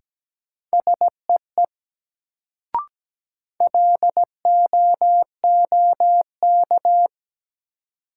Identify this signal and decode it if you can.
Morse code, heard as a keyed tone, about 17 wpm, 700 Hz. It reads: SEE LOOK
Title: Sets of 2 Words - New Word Drill - 17wpm